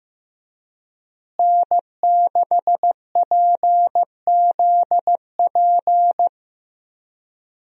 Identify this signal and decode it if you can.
Morse code, heard as a keyed tone, about 15 wpm, 700 Hz. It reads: N6PZP
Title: USA Call Signs - 5 Characters - Repeated 2x - 15wpm